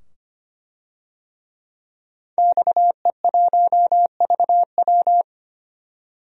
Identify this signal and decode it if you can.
Morse code, heard as a keyed tone, about 25 wpm, 700 Hz. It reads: XE1VW
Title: DXCC-100 Call Signs - 5 Characters - 25wpm